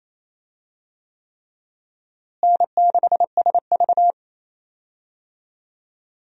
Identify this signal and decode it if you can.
Morse code, heard as a keyed tone, about 27 wpm, 700 Hz. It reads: N6SV
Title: USA Call Signs - 4 Characters - Repeated 3x - 28wpm